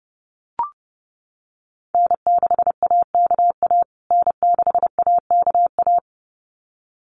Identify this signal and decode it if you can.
Morse code, heard as a keyed tone, about 30 wpm, 700 Hz. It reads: N6AKA N6AKA
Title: USA Call Signs Repeated 2x 30wpm